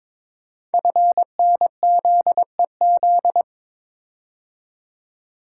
Morse code, heard as a keyed tone, about 23 wpm, 700 Hz. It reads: FNZEZ
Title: Code Groups - 5 Letters - 22wpm